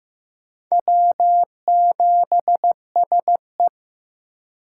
Morse code, heard as a keyed tone, about 16 wpm, 700 Hz. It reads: W7SE
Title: K1USN SST QSO Practice - 15wpm